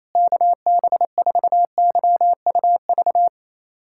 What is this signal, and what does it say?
Morse code, heard as a keyed tone, about 28 wpm, 700 Hz. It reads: KB4YUV
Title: USA Callsign DE Callsign 28wpm